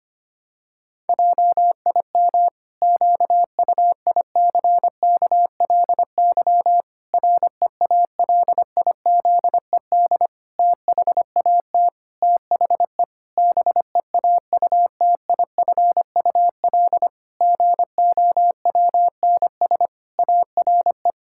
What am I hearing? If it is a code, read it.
Morse code, heard as a keyed tone, about 25 wpm, 700 Hz. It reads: JIM QUICKLY REALIZED THAT THE BEAUTIFUL GOWNS ARE
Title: Pangram Challenge v2 - 25wpm